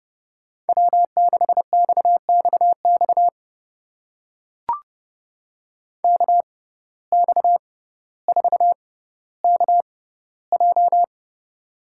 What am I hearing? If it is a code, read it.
Morse code, heard as a keyed tone, about 30 wpm, spaced as for 15 wpm, 700 Hz. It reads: W6XXX KX4KJ
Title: USA Call Signs - 5 Characters - 6x Character Spacing - 30wpm